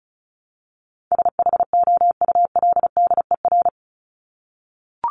Morse code, heard as a keyed tone, about 34 wpm, 700 Hz. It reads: SHOULDER
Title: Top 1501-1750 Words 35wpm